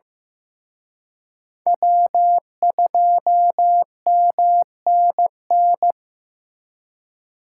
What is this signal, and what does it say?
Morse code, heard as a keyed tone, about 18 wpm, 700 Hz. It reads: W2MNN
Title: Speed Racing - USA Call Signs 15wpm